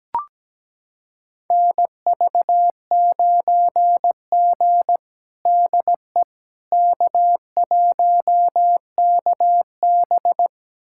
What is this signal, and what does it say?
Morse code, heard as a keyed tone, about 17 wpm, 700 Hz. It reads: NV9G DE K1KB
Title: USA Callsign DE Callsign - 4 Characters - 17wpm